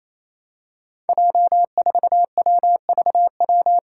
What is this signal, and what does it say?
Morse code, heard as a keyed tone, about 28 wpm, 700 Hz. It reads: J4WVW
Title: Five Challenging Letter-Number Combinations 28wpm